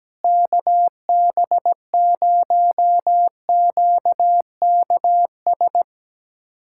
Morse code, heard as a keyed tone, about 17 wpm, 700 Hz. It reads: KB0QKS
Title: USA Callsign DE Callsign 17wpm